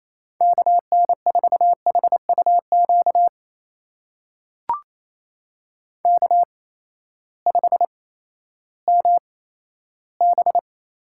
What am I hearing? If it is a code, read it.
Morse code, heard as a keyed tone, about 28 wpm, spaced as for 14 wpm, 700 Hz. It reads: KN4HUQ K5MB
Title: USA Call Signs - 8x Character Spacing - 28wpm